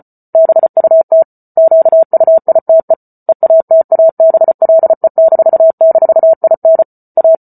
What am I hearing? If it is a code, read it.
Morse code, heard as a keyed tone, about 34 wpm, 650 Hz. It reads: BUT QUITE EATABLE--IN A